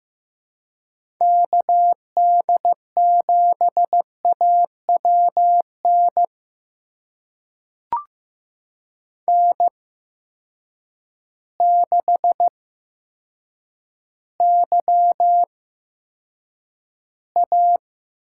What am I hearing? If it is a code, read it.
Morse code, heard as a keyed tone, about 15 wpm, spaced as for 6 wpm, 700 Hz. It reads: KD7AW NN6YA
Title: USA Call Signs - 8x Character Spacing - 15wpm